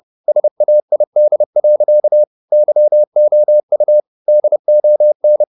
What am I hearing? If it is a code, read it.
Morse code, heard as a keyed tone, about 30 wpm, 600 Hz. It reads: SAID. YOU DON
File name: the old man and the sea / the old man and the sea 1 30wpm